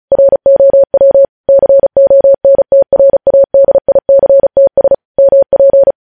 Morse code, heard as a keyed tone, about 36 wpm, 550 Hz. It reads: ROW CONTRADICTS MP
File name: podcast / News headlines-1x-35x35